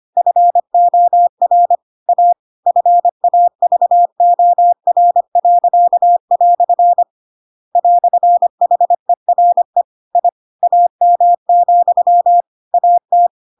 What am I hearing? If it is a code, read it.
Morse code, heard as a keyed tone, about 25 wpm, 700 Hz. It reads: FOR A FAVOR." "HERE I AM, AT
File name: Pinocchio / Pinocchio ch2 25 25 ik1hge